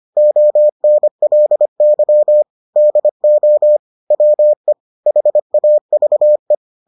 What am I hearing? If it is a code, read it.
Morse code, heard as a keyed tone, about 25 wpm, 600 Hz. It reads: ONLY DO WE HAVE